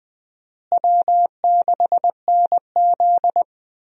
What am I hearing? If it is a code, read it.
Morse code, heard as a keyed tone, about 20 wpm, 700 Hz. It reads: W6NZ
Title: USA Call Signs - 4 Characters - Repeated 2x - 20wpm